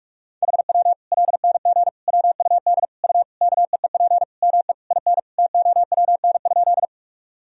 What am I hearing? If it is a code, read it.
Morse code, heard as a keyed tone, about 46 wpm, 700 Hz. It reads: HW LNG WUD U KEEP ME IN TOWN?
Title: Sentences from Top 500 Words and Abbreviations 45wpm